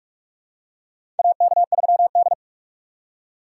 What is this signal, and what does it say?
Morse code, heard as a keyed tone, about 46 wpm, 700 Hz. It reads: AK3D
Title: USA Call Signs - 4 Characters - 4x Character Spacing - 45wpm